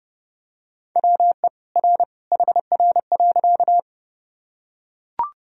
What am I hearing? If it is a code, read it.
Morse code, heard as a keyed tone, about 30 wpm, 700 Hz. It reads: WE R HR.
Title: Sentences from Top 200 Words and Abbreviations 30wpm